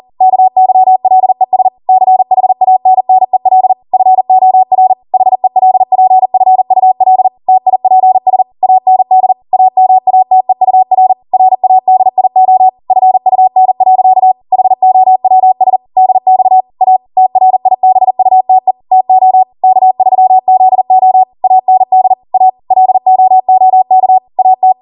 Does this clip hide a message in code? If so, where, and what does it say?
Yes: Morse code from 0.2 to 24.7 s, heard as a keyed tone, about 40 wpm, 750 Hz. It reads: KYLES CHANNEL FOR HELPFUL TIPS AND AMATEUR RADIO FUN. HOWS DX A TRIBUTE TO K3ZO AND A LOOK AT